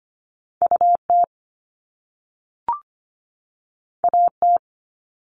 Morse code, heard as a keyed tone, about 25 wpm, 700 Hz. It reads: UT AT
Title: Top Two Letter Combinations in Words 25wpm